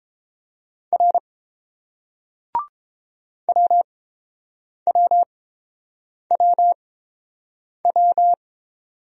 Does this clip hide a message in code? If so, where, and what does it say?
Yes: Morse code from 0.9 to 8.3 s, heard as a keyed tone, about 28 wpm, spaced as for 7 wpm, 700 Hz. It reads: R WWWW